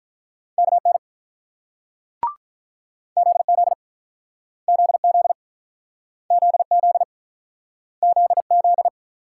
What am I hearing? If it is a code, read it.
Morse code, heard as a keyed tone, about 48 wpm, 700 Hz. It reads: KN ZZ ZZ ZZ ZZ